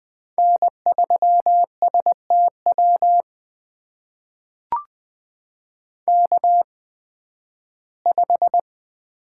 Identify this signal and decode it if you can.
Morse code, heard as a keyed tone, about 20 wpm, spaced as for 13 wpm, 700 Hz. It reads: N3STW K5
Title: USA Call Signs - 5 Characters - 8x Character Spacing - 20wpm